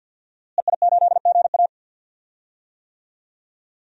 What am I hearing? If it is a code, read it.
Morse code, heard as a keyed tone, about 49 wpm, 700 Hz. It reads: EI8GA
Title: DXCC-100 Call Signs - 5 Characters - Repeated 3x - 50wpm